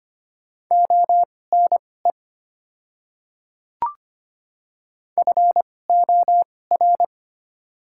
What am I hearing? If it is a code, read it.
Morse code, heard as a keyed tone, about 25 wpm, spaced as for 18 wpm, 690 Hz. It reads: ONE FOR
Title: 2x Character Spacing - Top 100 Words - 25wpm